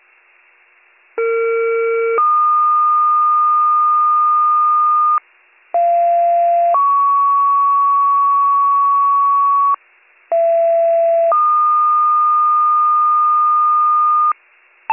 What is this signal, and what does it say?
Two-tone page, heard as a one-second tone, then a three-second tone, about 460 then 1220 Hz, about 690 then 1120 Hz, about 670 then 1220 Hz. It is heard from 1.2 to 14.3 s.